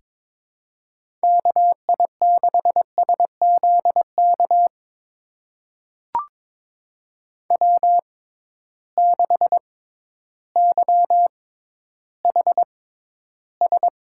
Morse code, heard as a keyed tone, about 21 wpm, spaced as for 12 wpm, 700 Hz. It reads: KI6SZK W6YHS
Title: USA Call Signs - 6x Character Spacing - 22wpm